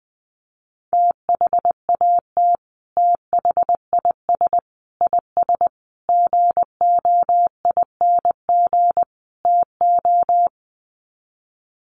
Morse code, heard as a keyed tone, about 20 wpm, 700 Hz. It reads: THAT THIS IS GOING TO
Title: Sets of 5 Words Encore2 20wpm